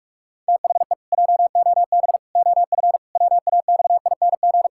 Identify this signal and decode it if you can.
Morse code, heard as a keyed tone, about 46 wpm, 700 Hz. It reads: THE JOB OF WAXING
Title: Pangram Challenge v2 - 45wpm